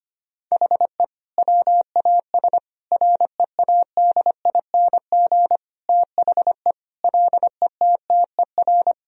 Morse code, heard as a keyed tone, about 25 wpm, 700 Hz. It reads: HE WAS READING THE LETTER
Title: Sentences from Top 1000 Words - Review Part 3 - 25wpm